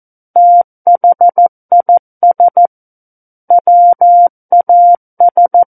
Morse code, heard as a keyed tone, about 14 wpm, 700 Hz. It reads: THIS WAS
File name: Book One / 05-Chapter 5 - 14 WPM